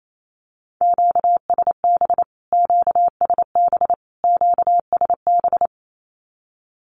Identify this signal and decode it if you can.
Morse code, heard as a keyed tone, about 28 wpm, 700 Hz. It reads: QSB QSB QSB
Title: Q-Codes Repeated 3x 28wpm